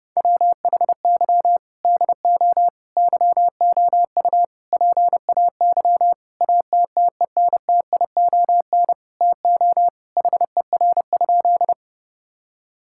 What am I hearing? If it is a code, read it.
Morse code, heard as a keyed tone, about 30 wpm, 700 Hz. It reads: WHY DO YOU PAY ATTENTION TO HER?
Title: Sentences from Top 1000 Words - Review Part 6 - 30wpm